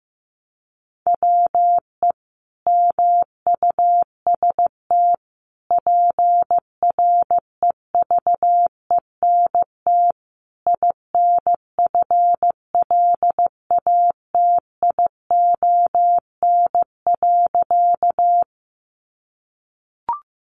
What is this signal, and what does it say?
Morse code, heard as a keyed tone, about 15 wpm, 700 Hz. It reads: WE MUST PREVENT INFLATION.
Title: Sentences from Top 1100 Words 15wpm